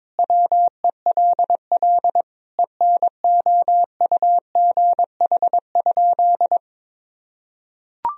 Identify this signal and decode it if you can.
Morse code, heard as a keyed tone, about 22 wpm, 700 Hz. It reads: WELL ENOUGH?
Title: Sentences from Top 1000 Words - Review Part 3 - 22wpm